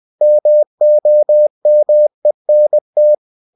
Morse code, heard as a keyed tone, about 20 wpm, 600 Hz. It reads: MOMENT
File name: cw-books / winnie-the-pooh / winnie-20wpm-page-0006